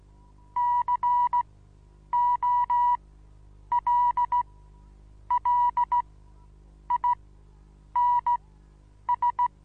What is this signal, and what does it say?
Morse code, heard as a keyed tone, about 16 wpm, spaced as for 8 wpm, 970 Hz. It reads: COLLINS
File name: Onlinelehrgang / morsen / DJ4UF Morsen25